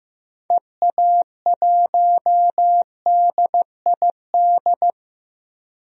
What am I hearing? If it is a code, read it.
Morse code, heard as a keyed tone, about 15 wpm, 700 Hz. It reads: EA1DID
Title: DXCC-100 Call Signs - 6 Characters - Repeated 2x - 15wpm